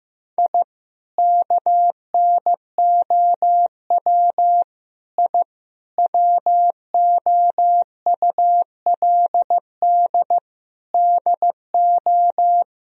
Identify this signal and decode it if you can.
Morse code, heard as a keyed tone, about 15 wpm, 700 Hz. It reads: I KNOW I WOULD DO
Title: Sentences from Top 1000 Words - Review Part 3 - 15wpm